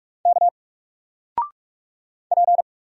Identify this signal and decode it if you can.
Morse code, heard as a keyed tone, about 44 wpm, 700 Hz. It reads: K P
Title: Single Letter-Number Alt 45wpm